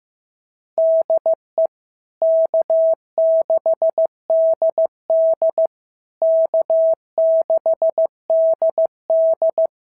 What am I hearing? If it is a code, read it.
Morse code, heard as a keyed tone, about 15 wpm, 650 Hz. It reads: DE K6DD K6DD
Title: SOTA QSO Practice 15wpm